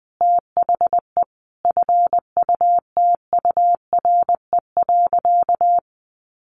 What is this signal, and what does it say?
Morse code, heard as a keyed tone, about 20 wpm, 700 Hz. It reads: THE FUTURE.
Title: A Sound Of Thunder 20wpm